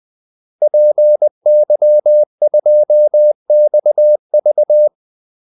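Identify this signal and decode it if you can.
Morse code, heard as a keyed tone, about 20 wpm, 600 Hz. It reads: PY2XV
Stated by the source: Morse call-sign practice at 20 wpm.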